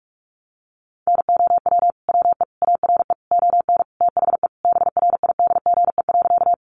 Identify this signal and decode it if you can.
Morse code, heard as a keyed tone, about 44 wpm, 700 Hz. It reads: NOW WE ARE ON THE BRIDGE.
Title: Sentences from Top 1200 Words 45wpm